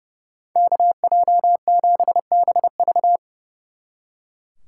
Morse code, heard as a keyed tone, about 35 wpm, 700 Hz. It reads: KJ7BV
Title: Speed Racing - USA 2x2 Call Signs 30wpm